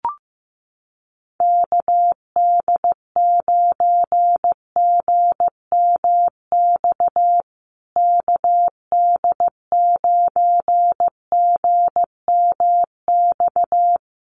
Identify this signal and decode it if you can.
Morse code, heard as a keyed tone, about 14 wpm, 700 Hz. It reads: KD9GMX KD9GMX
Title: USA Call Signs Repeated 3x 15wpm